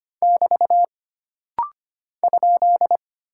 Morse code, heard as a keyed tone, about 25 wpm, 700 Hz. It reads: <BT> ?
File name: Single Punctuation-Prosign - Rapid-Fire - 25wpm